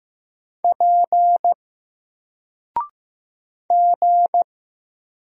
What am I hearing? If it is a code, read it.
Morse code, heard as a keyed tone, about 15 wpm, 700 Hz. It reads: P G
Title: Single Letter-Number 15wpm